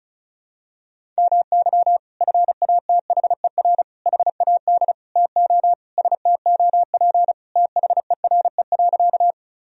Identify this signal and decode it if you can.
Morse code, heard as a keyed tone, about 36 wpm, 700 Hz. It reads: MY FATHER HAD TO STOP THERE.
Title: Sentences from Top 1000 Words - Review Part 1 - 35wpm